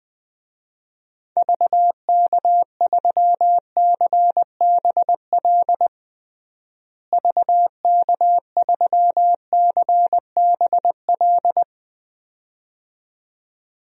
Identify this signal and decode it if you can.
Morse code, heard as a keyed tone, about 20 wpm, 700 Hz. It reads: VK3CBL VK3CBL
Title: DXCC-100 Call Signs - 6 Characters - Repeated 2x - 20wpm